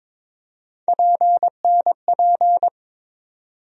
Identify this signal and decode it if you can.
Morse code, heard as a keyed tone, about 22 wpm, 700 Hz. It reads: PNP
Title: USA Call Sign Postfixes - 3 Letters 22wpm